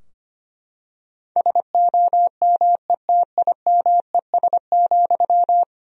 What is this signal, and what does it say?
Morse code, heard as a keyed tone, about 25 wpm, 700 Hz. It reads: SOMETIMES,
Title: Sentences from Top 1000 Words - Review Part 3 - 25wpm